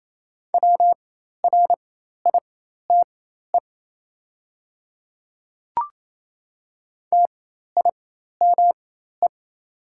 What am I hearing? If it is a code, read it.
Morse code, heard as a keyed tone, about 28 wpm, spaced as for 13 wpm, 700 Hz. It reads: WRITE TIME
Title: 4x Character Spacing - Top 100 Words - 28wpm